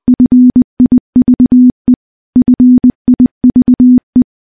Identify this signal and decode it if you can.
Morse code, heard as a keyed tone, about 20 wpm, 250 Hz. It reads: FIVE FIVE
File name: morse-training-20WPM